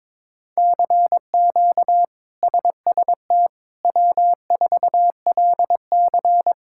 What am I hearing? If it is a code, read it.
Morse code, heard as a keyed tone, about 21 wpm, 700 Hz. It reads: CQ SST W4LC
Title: K1USN SST QSO Practice - 22wpm